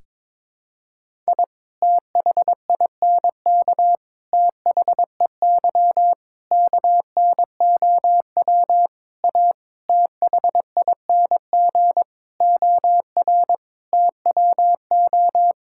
Morse code, heard as a keyed tone, about 22 wpm, 700 Hz. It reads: I THINK THEY KNOW A THING OR TWO
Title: Sentences from Top 1000 Words - Review Part 5 - 22wpm